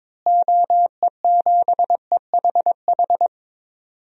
Morse code, heard as a keyed tone, about 21 wpm, 700 Hz. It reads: OE7EHH
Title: DXCC-100 Call Signs - 6 Characters - Repeated 3x - 22wpm